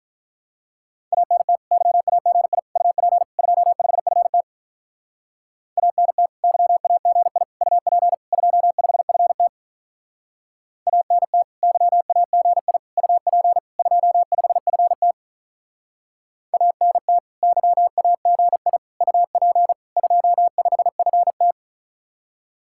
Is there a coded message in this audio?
Yes: Morse code from 1.1 to 21.5 s, heard as a keyed tone, about 47 wpm, 700 Hz. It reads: ANT YAGI UP 25FT ANT YAGI UP 25FT ANT YAGI UP 25FT ANT YAGI UP 25FT